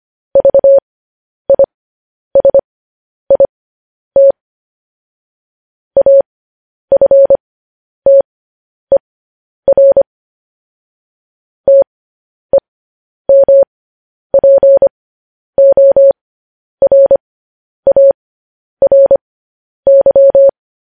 Morse code, heard as a keyed tone, about 25 wpm, spaced as for 10 wpm, 550 Hz. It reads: VISIT AFTER TEMPORARY
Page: news-headlines-in-morse-code-at-10-wpm